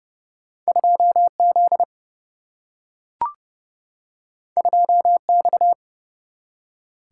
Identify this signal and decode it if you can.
Morse code, heard as a keyed tone, about 30 wpm, 700 Hz. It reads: 2 Z 2 X
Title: Single-Letter Single-Number 30wpm